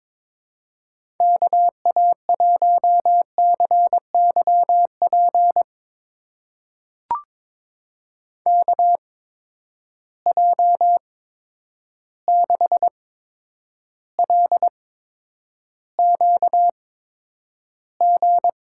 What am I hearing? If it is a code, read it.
Morse code, heard as a keyed tone, about 22 wpm, spaced as for 9 wpm, 700 Hz. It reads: KA1CYP KJ6LQG
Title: USA Call Signs - 6 Characters - 8x Character Spacing - 22wpm